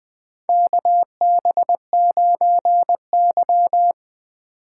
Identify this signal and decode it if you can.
Morse code, heard as a keyed tone, about 21 wpm, 700 Hz. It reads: KB9Y
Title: USA Call Signs - 4 Characters - 4x Character Spacing - 20wpm